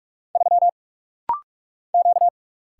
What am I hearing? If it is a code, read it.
Morse code, heard as a keyed tone, about 45 wpm, 700 Hz. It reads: 3 Q
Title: Single Letter-Number - Rapid-Fire 45wpm